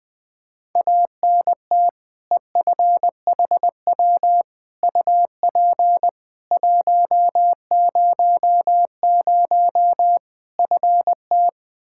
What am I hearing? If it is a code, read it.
Morse code, heard as a keyed tone, about 20 wpm, 700 Hz. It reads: ANT EFHW UP 100 FT
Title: Practice QSO - Exchange 2 - Info - 20wpm